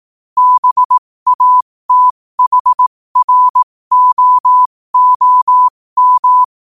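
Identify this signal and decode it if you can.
Morse code, heard as a keyed tone, about 18 wpm, spaced as for 15 wpm, 1000 Hz. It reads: BATHROOM
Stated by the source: Morse quote-of-the-day practice at 15 WPM from Morse Resource.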